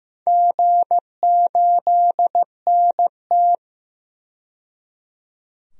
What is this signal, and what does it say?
Morse code, heard as a keyed tone, about 15 wpm, 700 Hz. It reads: G8NT
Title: DXCC-100 Call Signs - 4 Characters - 15wpm